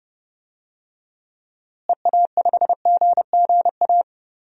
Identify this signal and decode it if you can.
Morse code, heard as a keyed tone, about 31 wpm, 700 Hz. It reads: EA5GGA
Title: DXCC-100 Call Signs - 6 Characters - Repeated 3x - 30wpm